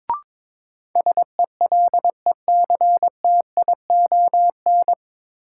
Morse code, heard as a keyed tone, about 22 wpm, 700 Hz. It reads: SELECTION
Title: Sentences from Top 1400 Words - New Word Drill - 22wpm